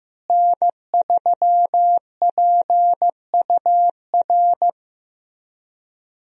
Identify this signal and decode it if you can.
Morse code, heard as a keyed tone, about 15 wpm, 700 Hz. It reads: N3PUR
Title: USA Call Signs - 5 Characters - Repeated 3x - 15wpm